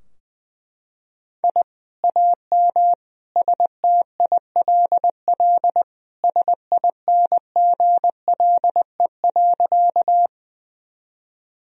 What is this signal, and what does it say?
Morse code, heard as a keyed tone, about 20 wpm, 700 Hz. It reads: I AM STILL SINGLE.
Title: Sentences from Top 1000 Words - Review Part 4 - 20wpm